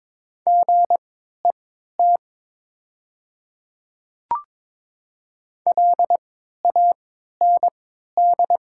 Morse code, heard as a keyed tone, about 22 wpm, spaced as for 12 wpm, 700 Hz. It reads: GET LAND
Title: 3x Character Spacing - Top 200 Words - 22wpm